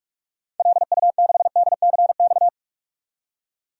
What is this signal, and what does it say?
Morse code, heard as a keyed tone, about 44 wpm, 700 Hz. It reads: LU6DCX
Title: DXCC-100 Call Signs - 6 Characters - 45wpm